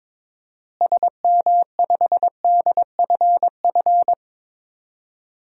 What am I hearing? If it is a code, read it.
Morse code, heard as a keyed tone, about 22 wpm, 700 Hz. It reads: SM5DFF
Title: DXCC-100 Call Signs - 6 Characters - Repeated 2x - 22wpm